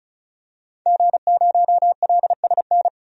Morse code, heard as a keyed tone, about 36 wpm, 700 Hz. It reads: G0LSN
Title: DXCC-100 Call Signs - 5 Characters - 35wpm